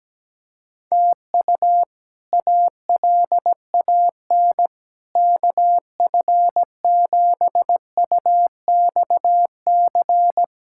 Morse code, heard as a keyed tone, about 17 wpm, 700 Hz. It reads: TU ALAN KF7UXC
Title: K1USN SST QSO Practice - 17wpm